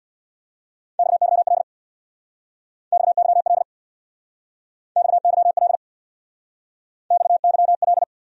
Morse code, heard as a keyed tone, about 66 wpm, 700 Hz. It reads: XYL XYL XYL XYL